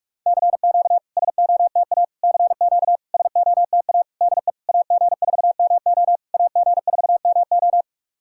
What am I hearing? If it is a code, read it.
Morse code, heard as a keyed tone, about 44 wpm, 690 Hz. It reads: CQ SOTA CQ SOTA DE AG4MO AG4MO
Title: SOTA QSO Practice 45wpm